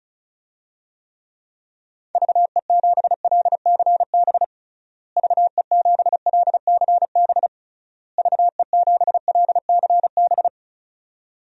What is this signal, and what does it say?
Morse code, heard as a keyed tone, about 34 wpm, 700 Hz. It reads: VE7LCB VE7LCB VE7LCB
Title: DXCC-100 Call Signs - 6 Characters - Repeated 3x - 35wpm